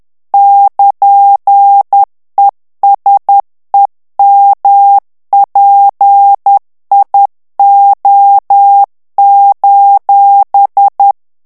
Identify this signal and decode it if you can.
Morse code, heard as a keyed tone, about 10 wpm, 800 Hz. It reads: <KN> ESEMPIO:
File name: ITALIEN texte clair 2 vitesse 10 mots minute